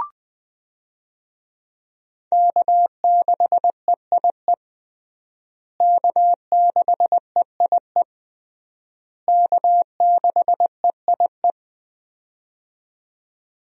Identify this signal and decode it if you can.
Morse code, heard as a keyed tone, about 20 wpm, 700 Hz. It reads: K6EIE K6EIE K6EIE
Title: USA Call Signs - 5 Characters - Repeated 3x - 20wpm